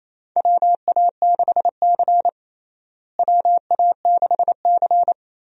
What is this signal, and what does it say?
Morse code, heard as a keyed tone, about 28 wpm, 700 Hz. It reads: WA6C WA6C
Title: USA Call Signs - 4 Characters - Repeated 3x - 28wpm